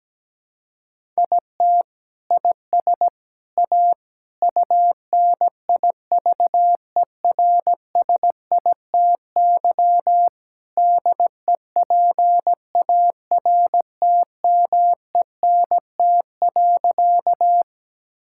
Morse code, heard as a keyed tone, about 17 wpm, 700 Hz. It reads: IT IS A UNIVERSITY DEPARTMENT.
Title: Sentences from Top 1000 Words - Review Part 5 - 17wpm